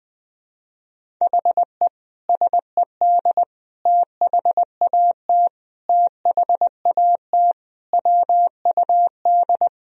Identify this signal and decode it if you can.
Morse code, heard as a keyed tone, about 20 wpm, 700 Hz. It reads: HE SED THAT THAT WUD
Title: Sentences from Top 200 Words and Abbreviations 20wpm